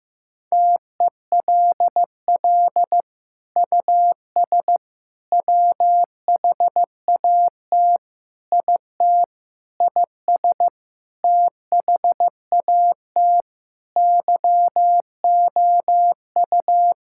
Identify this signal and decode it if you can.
Morse code, heard as a keyed tone, about 15 wpm, 700 Hz. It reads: TELL US WHAT IT IS THAT YOU
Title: Sentences from Top 1000 Words - Review Part 6 - 15wpm